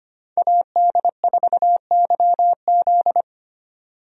Morse code, heard as a keyed tone, about 25 wpm, 690 Hz. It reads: AD4YZ
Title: USA Call Signs - 5 Characters - 25wpm